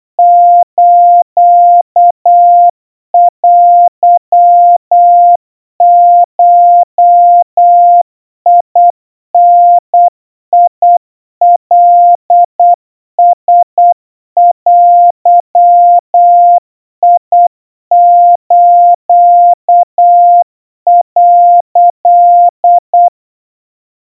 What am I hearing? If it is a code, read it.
Morse code (Japanese Wabun code), heard as a keyed tone, about 8 wpm, 700 Hz. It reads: ステゴダカラデス」
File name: 30 cpm 700 Hz Crispy